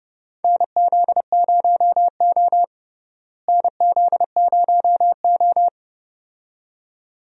Morse code, heard as a keyed tone, about 31 wpm, 700 Hz. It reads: NZ0O NZ0O
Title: USA Call Signs - 4 Characters - Repeated 2x - 30wpm